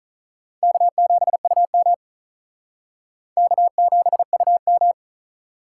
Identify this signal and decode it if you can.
Morse code, heard as a keyed tone, about 39 wpm, 700 Hz. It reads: K7UM K7UM
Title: Speed Racing - USA 1x2 Call Signs 35wpm